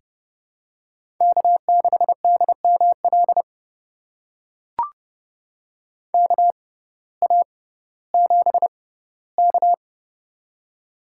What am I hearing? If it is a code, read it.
Morse code, heard as a keyed tone, about 32 wpm, spaced as for 17 wpm, 700 Hz. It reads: K6DML KA7K
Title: USA Call Signs - 6x Character Spacing - 30wpm